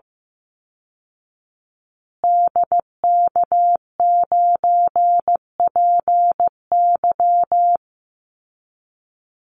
Morse code, heard as a keyed tone, about 15 wpm, 700 Hz. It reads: DK9PY
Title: Call Signs Repeated 2x - 15wpm